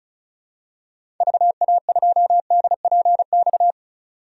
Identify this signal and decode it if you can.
Morse code, heard as a keyed tone, about 36 wpm, 700 Hz. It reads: VA2DPX
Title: DXCC-100 Call Signs - 6 Characters - 35wpm